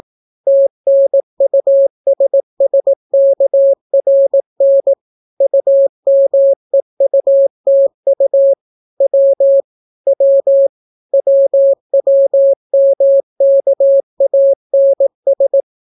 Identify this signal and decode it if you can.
Morse code, heard as a keyed tone, about 18 wpm, 550 Hz. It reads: TNUSSKRN UMEUTU W W WWMKANS